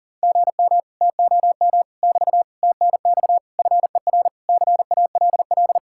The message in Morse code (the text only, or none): GM TOM <BT> TNX FER CALL